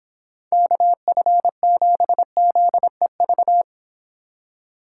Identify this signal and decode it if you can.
Morse code, heard as a keyed tone, about 26 wpm, 700 Hz. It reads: KF7ZEV